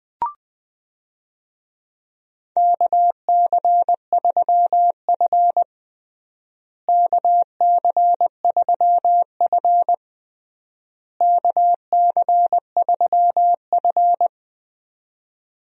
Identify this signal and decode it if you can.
Morse code, heard as a keyed tone, about 20 wpm, 690 Hz. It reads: KC3F KC3F KC3F
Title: USA Call Signs - 4 Characters - Repeated 3x - 20wpm